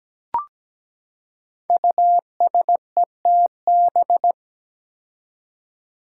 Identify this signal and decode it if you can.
Morse code, heard as a keyed tone, about 17 wpm, 700 Hz. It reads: USETB
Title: Five Challenging Letter-Number Combinations 17wpm